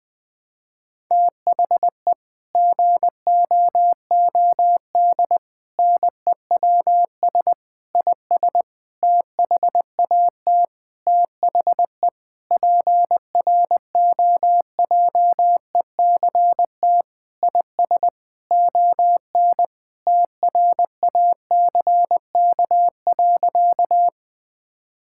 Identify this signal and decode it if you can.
Morse code, heard as a keyed tone, about 20 wpm, 700 Hz. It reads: THE GOOD NEWS IS THAT THE PROJECT IS ON TRACK.
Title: Sentences with 6-Word Phrases - 20wpm